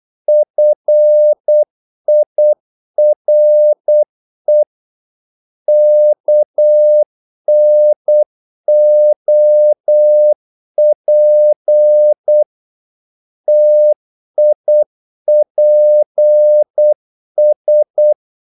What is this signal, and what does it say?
Morse code, heard as a keyed tone, about 8 wpm, 600 Hz. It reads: FIRE KNOP TIPS
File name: Korte ord DK 8wpm